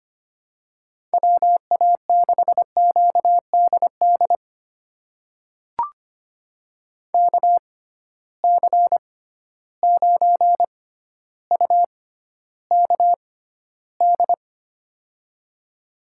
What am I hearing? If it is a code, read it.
Morse code, heard as a keyed tone, about 25 wpm, spaced as for 13 wpm, 700 Hz. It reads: WA6QDD KC9UKD
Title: USA Call Signs - 6 Characters - 6x Character Spacing - 25wpm